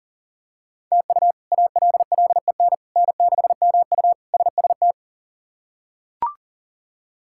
Morse code, heard as a keyed tone, about 40 wpm, 700 Hz. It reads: TU ALLEN N6MU SST
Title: K1USN SST QSO Practice - 40wpm